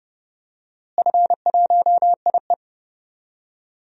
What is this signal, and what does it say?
Morse code, heard as a keyed tone, about 31 wpm, 700 Hz. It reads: F1IE
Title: DXCC-100 Call Signs - 4 Characters - Repeated 3x - 30wpm